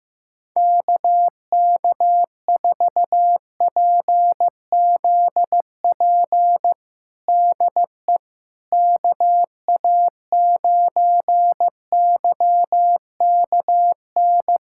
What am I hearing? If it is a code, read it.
Morse code, heard as a keyed tone, about 15 wpm, 700 Hz. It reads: KK4PZP DE KA9YKN
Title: USA Callsign DE Callsign - 6 Characters - 15wpm